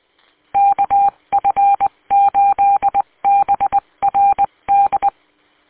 Morse code, heard as a keyed tone, about 20 wpm, 800 Hz. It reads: KF8BRD
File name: w8cmn- -combined